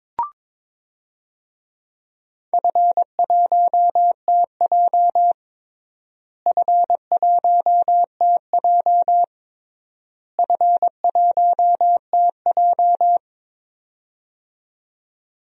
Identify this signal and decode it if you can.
Morse code, heard as a keyed tone, about 22 wpm, 700 Hz. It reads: F1TJ F1TJ F1TJ
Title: DXCC-100 Call Signs - 4 Characters - Repeated 3x - 22wpm